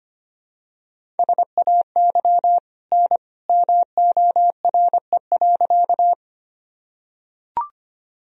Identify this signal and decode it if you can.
Morse code, heard as a keyed tone, about 25 wpm, 700 Hz. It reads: SAY N MORE.
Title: Sentences from Top 200 Words and Abbreviations 25wpm